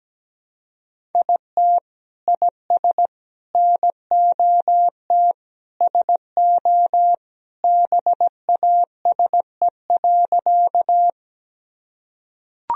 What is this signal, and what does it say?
Morse code, heard as a keyed tone, about 17 wpm, 700 Hz. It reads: IT IS NOT SO BASE.
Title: Sentences from Top 1000 Words - Review Part 1 - 17wpm